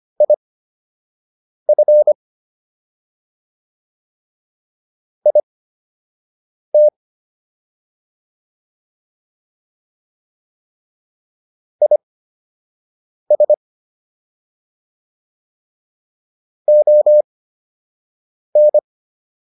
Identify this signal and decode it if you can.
Morse code, heard as a keyed tone, about 25 wpm, spaced as for 6 wpm, 600 Hz. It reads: IF IT IS ON